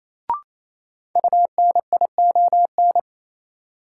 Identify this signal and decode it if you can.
Morse code, heard as a keyed tone, about 28 wpm, 700 Hz. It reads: UNION
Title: Sentences from Top 1400 Words - New Word Drill - 28wpm